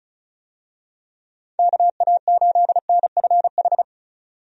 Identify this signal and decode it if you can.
Morse code, heard as a keyed tone, about 34 wpm, 700 Hz. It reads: KA8NFH
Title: USA Call Signs - 6 Characters - 35wpm